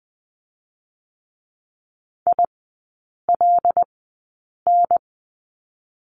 Morse code, heard as a keyed tone, about 20 wpm, spaced as for 8 wpm, 700 Hz. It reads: ILN
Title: 081 Lesson 14e - Number 2 - Three Characters 20wpm